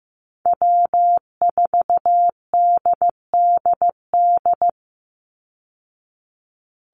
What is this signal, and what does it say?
Morse code, heard as a keyed tone, about 15 wpm, 700 Hz. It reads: W4DDD
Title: Call Signs Repeated 2x - 15wpm